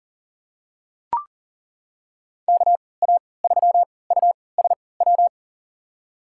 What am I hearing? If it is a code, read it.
Morse code, heard as a keyed tone, about 40 wpm, spaced as for 23 wpm, 700 Hz. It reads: KA3USW
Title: USA Call Signs - 6 Characters - 3x Character Spacing - 40wpm